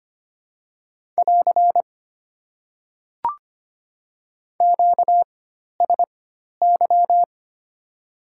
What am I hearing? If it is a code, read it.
Morse code, heard as a keyed tone, about 25 wpm, spaced as for 12 wpm, 700 Hz. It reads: <AR> QSY
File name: Top 90 QSO Elements - 4x Character Spacing - 25wpm